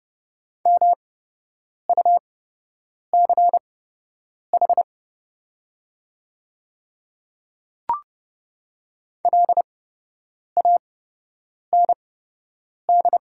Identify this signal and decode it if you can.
Morse code, heard as a keyed tone, about 32 wpm, spaced as for 8 wpm, 700 Hz. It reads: MUCH LAND
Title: 8x Character Spacing - Top 300 Words - 30wpm